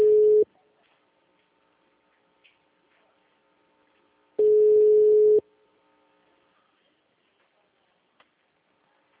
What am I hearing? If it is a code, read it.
Ringback tone (Europe, 425 Hz).
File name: out- -2018